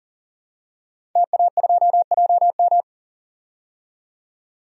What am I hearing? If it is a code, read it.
Morse code, heard as a keyed tone, about 40 wpm, 700 Hz. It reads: TA2JM
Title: DXCC-100 Call Signs - 5 Characters - 40wpm